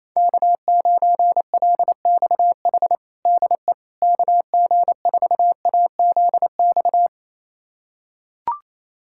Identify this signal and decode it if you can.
Morse code, heard as a keyed tone, about 28 wpm, 700 Hz. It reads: K9LXH DE KG4AZX
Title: USA Callsign DE Callsign 28wpm